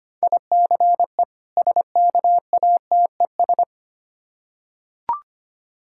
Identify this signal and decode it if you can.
Morse code, heard as a keyed tone, about 25 wpm, 700 Hz. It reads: ICE SKATES
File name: practice / 25wpm / Difficult Words with Hints - 25wpm